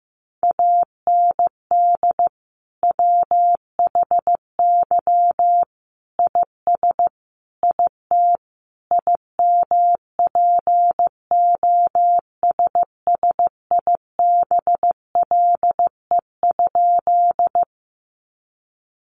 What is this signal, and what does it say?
Morse code, heard as a keyed tone, about 15 wpm, 700 Hz. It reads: AND WHY IS IT IMPOSSIBLE?
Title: Sentences from Top 1100 Words 15wpm